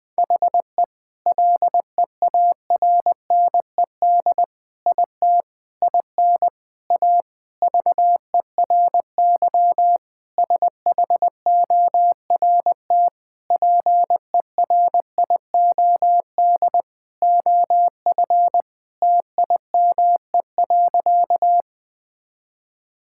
Morse code, heard as a keyed tone, about 20 wpm, 700 Hz. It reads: HE LEARNED IT IN A VERY SHORT PERIOD OF TIME.
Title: Sentences with 7-Word Phrases - Spoken Beginning - 20wpm